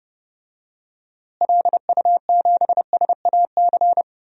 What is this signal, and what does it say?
Morse code, heard as a keyed tone, about 32 wpm, 700 Hz. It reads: LU7SAC
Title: DXCC-100 Call Signs - 6 Characters - Repeated 3x - 30wpm